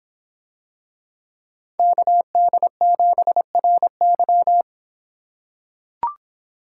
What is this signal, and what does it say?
Morse code, heard as a keyed tone, about 26 wpm, 700 Hz. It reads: KD7RY